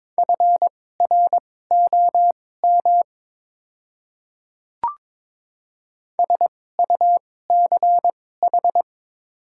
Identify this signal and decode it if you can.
Morse code, heard as a keyed tone, about 22 wpm, spaced as for 16 wpm, 700 Hz. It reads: FROM SUCH
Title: 2x Character Spacing - Top 300 Words - 22wpm